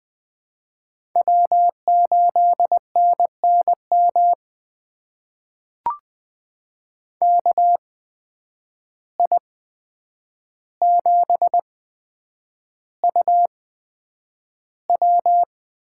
Morse code, heard as a keyed tone, about 20 wpm, spaced as for 9 wpm, 700 Hz. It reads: W8NNM KI7UW
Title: USA Call Signs - 5 Characters - 8x Character Spacing - 20wpm